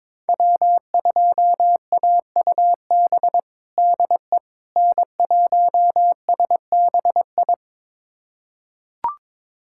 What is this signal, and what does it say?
Morse code, heard as a keyed tone, about 22 wpm, 700 Hz. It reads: W2AUB DE N1SBI
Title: USA Callsign DE Callsign 22wpm